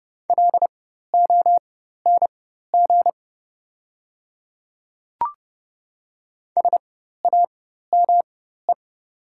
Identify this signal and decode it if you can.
Morse code, heard as a keyed tone, about 29 wpm, spaced as for 14 wpm, 690 Hz. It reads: LONG SAME
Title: 4x Character Spacing - Top 200 Words - 30wpm